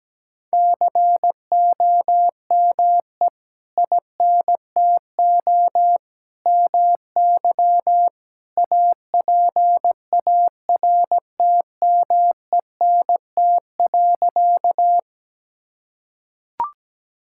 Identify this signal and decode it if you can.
Morse code, heard as a keyed tone, about 17 wpm, 700 Hz. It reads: COME INTO MY APARTMENT.
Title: Sentences from Top 1000 Words - Review Part 6 - 17wpm